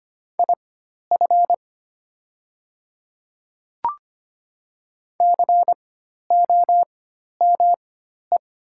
Morse code, heard as a keyed tone, about 25 wpm, spaced as for 12 wpm, 700 Hz. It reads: IF COME
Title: 4x Character Spacing - Top 100 Words - 25wpm